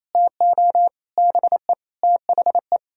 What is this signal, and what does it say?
Morse code, heard as a keyed tone, about 34 wpm, 700 Hz. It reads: TO BE THE
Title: Speed-Racing - Sets of 5 Words 28wpm